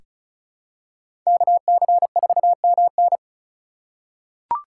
Morse code, heard as a keyed tone, about 36 wpm, 700 Hz. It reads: KC4MN
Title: USA Call Signs - 5 Characters - 4x Character Spacing - 35wpm